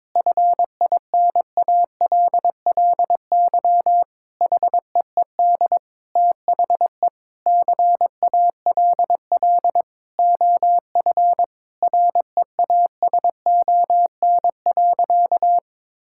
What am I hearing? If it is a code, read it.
Morse code, heard as a keyed tone, about 22 wpm, 700 Hz. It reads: FINALLY HEED THE CALL OF REASON.
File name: practice / 22wpm / Sentences with 8-Word Phrases - 22wpm